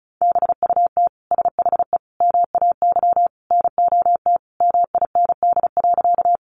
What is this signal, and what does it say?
Morse code, heard as a keyed tone, about 34 wpm, 700 Hz. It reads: BUT SHE MAY NOT MIND.
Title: Sentences from Top 600 Words 35wpm